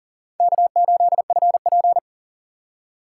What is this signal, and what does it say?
Morse code, heard as a keyed tone, about 40 wpm, 700 Hz. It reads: K8FP
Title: USA Call Signs - 3x Character Spacing - 40wpm